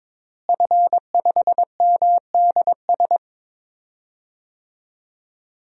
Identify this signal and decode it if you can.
Morse code, heard as a keyed tone, about 22 wpm, 700 Hz. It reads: F5MDS
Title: DXCC-100 Call Signs - 5 Characters - Repeated 3x - 22wpm